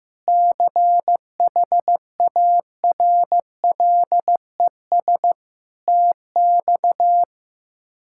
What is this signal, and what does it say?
Morse code, heard as a keyed tone, about 15 wpm, 700 Hz. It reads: CHARLES TX
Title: K1USN SST QSO Practice - 15wpm